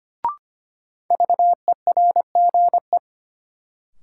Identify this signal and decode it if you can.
Morse code, heard as a keyed tone, about 25 wpm, 700 Hz. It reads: VERGE
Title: Sets of 4 Words - New Word Drill - 25wpm